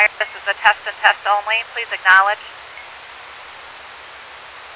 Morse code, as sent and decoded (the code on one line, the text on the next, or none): none